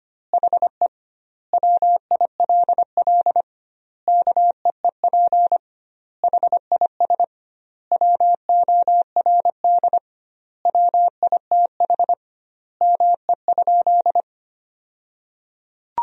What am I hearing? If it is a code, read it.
Morse code, heard as a keyed tone, about 25 wpm, 700 Hz. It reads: HE WILL KEEP HIS WORD WITH ME?
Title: 2x Word-Spacing - Sentences from Top 300 Words - 25wpm